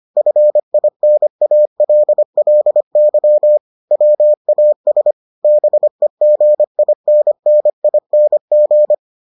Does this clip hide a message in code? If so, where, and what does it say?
Yes: Morse code from 0.2 to 8.9 s, heard as a keyed tone, about 25 wpm, 600 Hz. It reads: FINALLY WAS BEGINNING